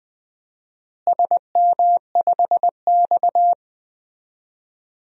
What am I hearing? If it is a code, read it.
Morse code, heard as a keyed tone, about 21 wpm, 700 Hz. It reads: SM5X
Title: DXCC-100 Call Signs - 4 Characters - Repeated 2x - 20wpm